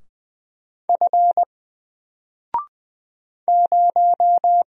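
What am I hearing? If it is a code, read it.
Morse code, heard as a keyed tone, about 20 wpm, 700 Hz. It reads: F 0
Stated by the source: Single letter and number Morse drill at 20 wpm, from N4LCD.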